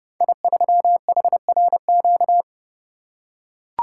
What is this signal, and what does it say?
Morse code, heard as a keyed tone, about 30 wpm, 700 Hz. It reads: I3HRQ
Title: Code Groups - 5 Letter:Numbers - 30wpm